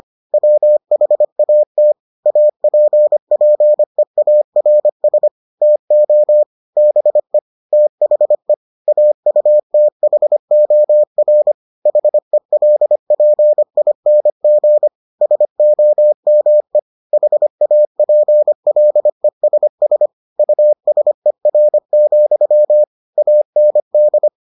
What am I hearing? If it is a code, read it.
Morse code, heard as a keyed tone, about 25 wpm, 600 Hz. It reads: WHAT APPEARS TO BE THE AUTHOR HELPING SOME HAPLESS USER, AND